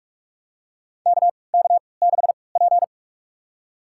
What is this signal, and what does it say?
Morse code, heard as a keyed tone, about 42 wpm, spaced as for 25 wpm, 700 Hz. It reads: KK6P
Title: USA Call Signs - 4 Characters - 3x Character Spacing - 45wpm